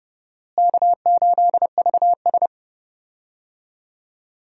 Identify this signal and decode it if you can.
Morse code, heard as a keyed tone, about 30 wpm, 700 Hz. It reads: K8VS
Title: USA Call Signs - 4 Characters - Repeated 3x - 30wpm